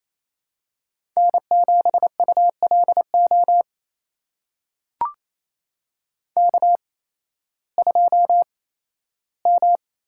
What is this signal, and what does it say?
Morse code, heard as a keyed tone, about 28 wpm, spaced as for 15 wpm, 700 Hz. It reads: N7ULO K2M
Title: USA Call Signs - 5 Characters - 8x Character Spacing - 28wpm